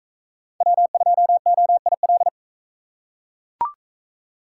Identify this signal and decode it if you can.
Morse code, heard as a keyed tone, about 41 wpm, 700 Hz. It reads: W2OIL